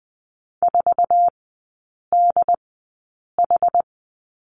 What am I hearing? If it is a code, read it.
Morse code, heard as a keyed tone, about 20 wpm, spaced as for 8 wpm, 700 Hz. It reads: 4DH
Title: 074 Lesson 13e - Letter L - Three Characters 20wpm